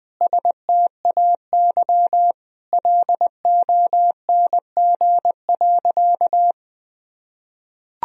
Morse code, heard as a keyed tone, about 20 wpm, 700 Hz. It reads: STAY LONG.
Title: Sentences from Top 1000 Words - Review Part 3 - 20wpm